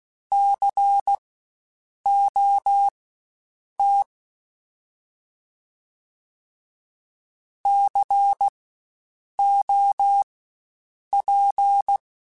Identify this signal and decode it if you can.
Morse code, heard as a keyed tone, about 16 wpm, spaced as for 7 wpm, 800 Hz. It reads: COT COP
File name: Spaced Alphas / Spaced Alphas mp3s / Cs sp @15